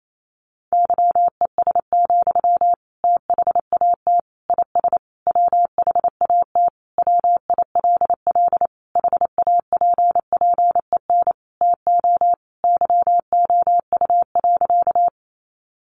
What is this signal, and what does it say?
Morse code, heard as a keyed tone, about 28 wpm, 700 Hz. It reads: YES, THAT IS WHAT WILL HAPPEN TO YOU.
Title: Sentences from Top 500 Words 28wpm